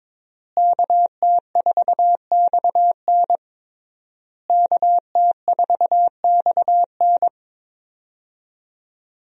Morse code, heard as a keyed tone, about 22 wpm, 700 Hz. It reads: KT4XN KT4XN
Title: USA Call Signs - 5 Characters - Repeated 3x - 22wpm